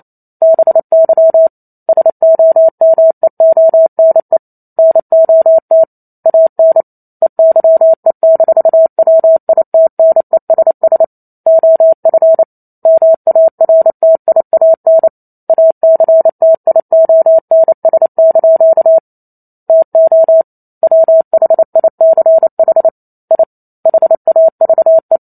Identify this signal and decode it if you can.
Morse code, heard as a keyed tone, about 28 wpm, 650 Hz. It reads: BY SOMEONE NOT AN EYE-WITNESS OF MARTIAN ACTIONS) TO WHICH I HAVE